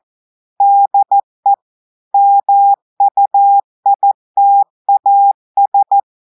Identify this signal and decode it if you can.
Morse code, heard as a keyed tone, about 14 wpm, 800 Hz. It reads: DE MUITAS
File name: audios / portugais / portugais texte en clair 1 vitesse 14 mots minutes